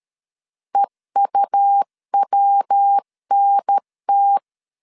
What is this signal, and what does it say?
Morse code, heard as a keyed tone, about 13 wpm, 800 Hz. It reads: EUWNT
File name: CW / DET / 1 3 1500C S A 12wpm 2905